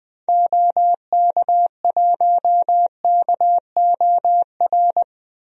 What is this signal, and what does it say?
Morse code, heard as a keyed tone, about 20 wpm, 700 Hz. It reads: OK1KOR
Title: DXCC-100 Call Signs - 6 Characters - 20wpm